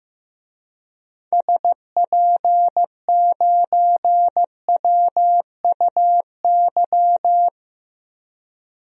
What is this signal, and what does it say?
Morse code, heard as a keyed tone, about 15 wpm, 700 Hz. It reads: SP9WUY
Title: DXCC-100 Call Signs - 6 Characters - Repeated 2x - 15wpm